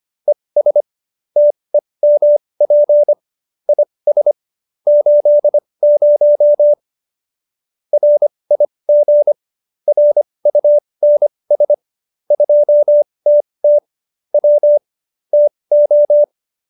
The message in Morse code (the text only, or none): ES TEMP IS 80 RIG RUNS 2TT W TO